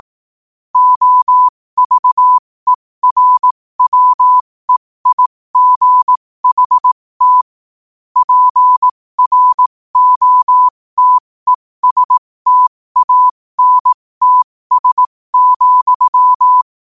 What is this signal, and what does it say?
Morse code, heard as a keyed tone, about 18 wpm, spaced as for 15 wpm, 1000 Hz. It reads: OVERWEIGHT PROTESTANTS,
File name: qotdfiles / 15wpm